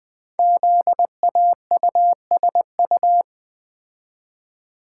Morse code, heard as a keyed tone, about 20 wpm, 700 Hz. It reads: ZAUSU
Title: Code Groups - 5 Letters - 20wpm